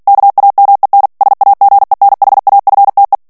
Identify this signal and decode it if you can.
Morse code, heard as a keyed tone, about 48 wpm, 800 Hz. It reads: KAMEN SAGENHAFTE
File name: morse mp3 / ALLEMAND texte clair 2 vitesse 45 mots minute